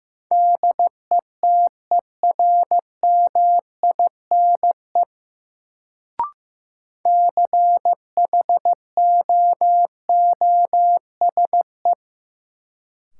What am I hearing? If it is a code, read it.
Morse code, heard as a keyed tone, about 15 wpm, 700 Hz. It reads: DETERMINE CHOOSE
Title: Sentences from Top 700 Words - New Word Drill - 15wpm